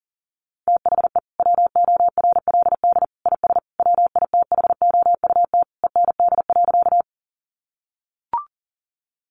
Morse code, heard as a keyed tone, about 40 wpm, 700 Hz. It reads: THE WORLD IS WITHOUT END.
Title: Sentences from Top 600 Words 40wpm